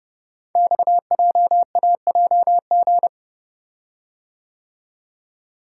Morse code, heard as a keyed tone, about 30 wpm, 700 Hz. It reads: XJAJG
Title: Code Groups - 5 Letters - 30wpm